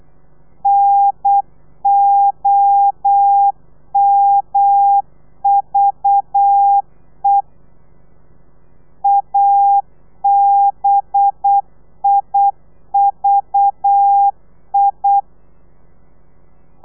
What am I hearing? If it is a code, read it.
Morse code, heard as a keyed tone, about 8 wpm, 800 Hz. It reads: NOMVE ABIVI
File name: telegrafia / i2tfj / Lez4 35